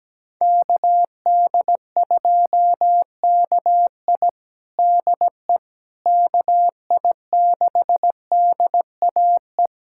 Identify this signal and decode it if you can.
Morse code, heard as a keyed tone, about 17 wpm, 700 Hz. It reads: KD2KI DE KI6DAE
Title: USA Callsign DE Callsign 17wpm